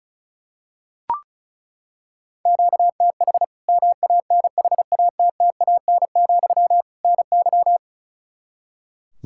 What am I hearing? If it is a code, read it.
Morse code, heard as a keyed tone, about 36 wpm, 700 Hz. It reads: QTH MANHATTAN, NY
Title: Practice QSO - Exchange 1 - Beginning - 35wpm